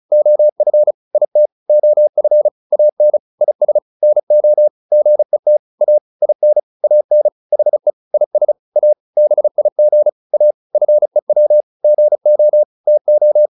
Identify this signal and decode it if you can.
Morse code, heard as a keyed tone, about 34 wpm, 600 Hz. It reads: OF IT OF AN IS NO GET A IN AN HE IS A BIG A FEW GO TO